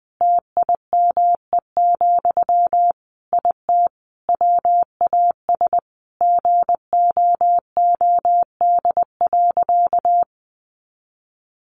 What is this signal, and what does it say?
Morse code, heard as a keyed tone, about 20 wpm, 700 Hz. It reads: TIME, IT WAS GOOD.
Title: Sentences from Top 1300 Words 20wpm